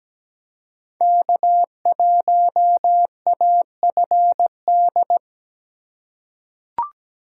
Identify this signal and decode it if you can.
Morse code, heard as a keyed tone, about 17 wpm, 700 Hz. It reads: K1AFD
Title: USA Call Signs - 3x Character Spacing - 17wpm